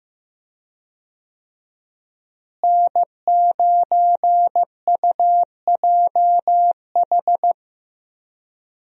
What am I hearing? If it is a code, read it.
Morse code, heard as a keyed tone, about 16 wpm, 700 Hz. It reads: N9UJH
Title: USA Call Signs - 5 Characters - Repeated 3x - 15wpm